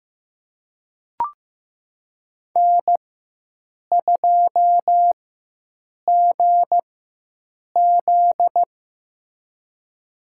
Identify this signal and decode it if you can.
Morse code, heard as a keyed tone, about 15 wpm, spaced as for 7 wpm, 700 Hz. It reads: N2GZ
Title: USA Call Signs - 4 Characters - 4x Character Spacing - 15wpm